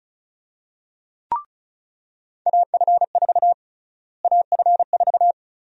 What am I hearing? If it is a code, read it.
Morse code, heard as a keyed tone, about 34 wpm, 700 Hz. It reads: AF4 AF4
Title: USA Call Sign Prefixes - Repeated 2x - 35wpm